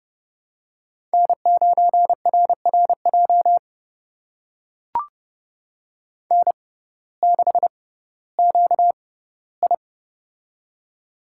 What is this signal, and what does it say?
Morse code, heard as a keyed tone, about 32 wpm, spaced as for 17 wpm, 700 Hz. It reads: N9RRJ N6QI
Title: USA Call Signs - 6x Character Spacing - 30wpm